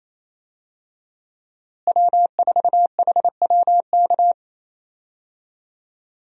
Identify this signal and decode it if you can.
Morse code, heard as a keyed tone, about 27 wpm, 700 Hz. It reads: W4HWK
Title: USA Call Signs - 5 Characters - Repeated 2x - 28wpm